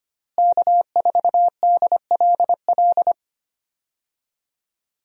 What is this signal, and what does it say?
Morse code, heard as a keyed tone, about 25 wpm, 700 Hz. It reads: K4DLL
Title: USA Call Signs - 5 Characters - Repeated 2x - 25wpm